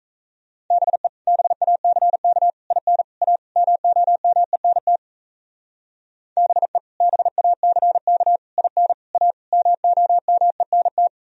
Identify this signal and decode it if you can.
Morse code, heard as a keyed tone, about 33 wpm, 700 Hz. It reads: BE BACK IN A MOMENT BE BACK IN A MOMENT
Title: Speed-Racing - Sets of 5 Words 28wpm